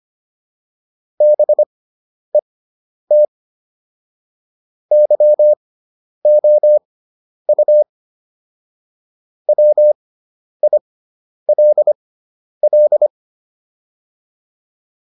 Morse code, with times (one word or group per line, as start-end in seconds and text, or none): 1.2-3.2 BET
4.9-7.8 YOU
9.5-13.1 WILL